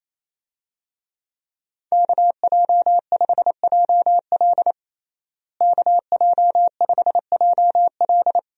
Morse code, heard as a keyed tone, about 27 wpm, 700 Hz. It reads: KJ5JL KJ5JL
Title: USA Call Signs - 5 Characters - Repeated 3x - 28wpm